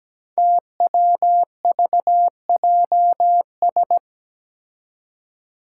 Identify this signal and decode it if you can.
Morse code, heard as a keyed tone, about 17 wpm, 700 Hz. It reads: TWVJS
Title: Five Challenging Letter-Number Combinations 17wpm